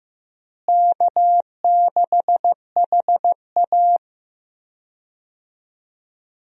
Morse code, heard as a keyed tone, about 15 wpm, 700 Hz. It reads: K6HA
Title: USA Call Signs - 4 Characters - Repeated 3x - 15wpm